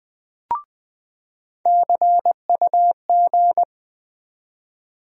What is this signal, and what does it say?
Morse code, heard as a keyed tone, about 20 wpm, 700 Hz. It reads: CUG
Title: USA Call Sign Postfixes - 3 Letters 20wpm